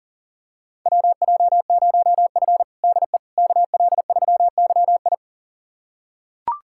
Morse code, heard as a keyed tone, about 40 wpm, 700 Hz. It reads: WJ0F DE KL3YI
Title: USA Callsign DE Callsign 40wpm